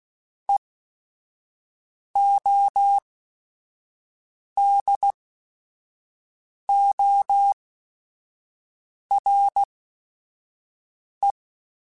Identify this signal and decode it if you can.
Morse code, heard as a keyed tone, about 16 wpm, spaced as for 5 wpm, 800 Hz. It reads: EODORE